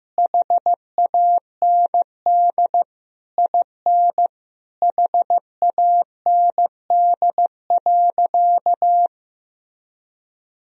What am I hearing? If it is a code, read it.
Morse code, heard as a keyed tone, about 15 wpm, 700 Hz. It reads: HAND IN HAND.
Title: Sentences from Top 1000 Words - Review Part 3 - 15wpm